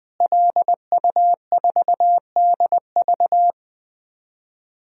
Morse code, heard as a keyed tone, about 20 wpm, 700 Hz. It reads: LU4DV
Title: DXCC-100 Call Signs - 5 Characters - Repeated 2x - 20wpm